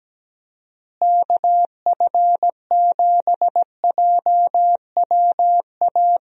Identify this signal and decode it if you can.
Morse code, heard as a keyed tone, about 17 wpm, 700 Hz. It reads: KF7JWA
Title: USA Call Signs - 6 Characters - Repeated 2x - 17wpm